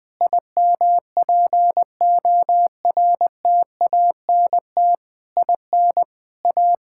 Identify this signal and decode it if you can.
Morse code, heard as a keyed tone, about 21 wpm, 700 Hz. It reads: IMPORTANT IN A
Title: Sentences from Top 1000 Words - Review Part 5 - 20wpm